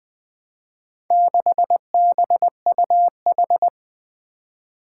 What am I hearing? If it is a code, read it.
Morse code, heard as a keyed tone, about 20 wpm, 700 Hz. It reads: 6BUH
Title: Four Challenging Letter-Number Combinations 20wpm